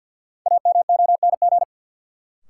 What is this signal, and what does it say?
Morse code, heard as a keyed tone, about 53 wpm, 700 Hz. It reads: AMONG
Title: Sets of 2 Words - New Word Drill - 50wpm